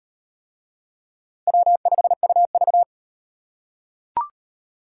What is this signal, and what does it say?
Morse code, heard as a keyed tone, about 38 wpm, 700 Hz. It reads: W5UV